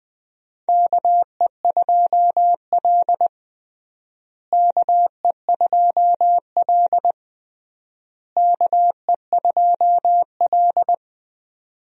Morse code, heard as a keyed tone, about 21 wpm, 700 Hz. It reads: KE2L KE2L KE2L
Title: USA Call Signs - 4 Characters - Repeated 3x - 20wpm